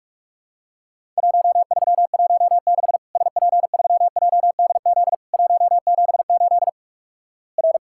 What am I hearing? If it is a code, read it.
Morse code, heard as a keyed tone, about 46 wpm, 700 Hz. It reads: 1316 SP3JDZ 178 R
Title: Worked All Europe DX Contest - QTC Exchange - 45wpm